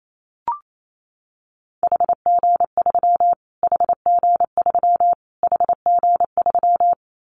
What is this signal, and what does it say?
Morse code, heard as a keyed tone, about 28 wpm, 700 Hz. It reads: HG3 HG3 HG3
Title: Callsign Prefixes for Top 25 DXCC - Repeated 3x - 28wpm